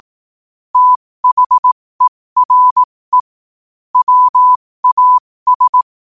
Morse code, heard as a keyed tone, about 18 wpm, spaced as for 15 wpm, 1000 Hz. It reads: THERE WAS
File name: qotdfiles / 15wpm